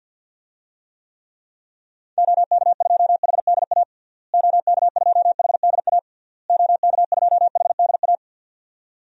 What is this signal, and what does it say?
Morse code, heard as a keyed tone, about 52 wpm, 700 Hz. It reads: OK2HDA OK2HDA OK2HDA
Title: DXCC-100 Call Signs - 6 Characters - Repeated 3x - 50wpm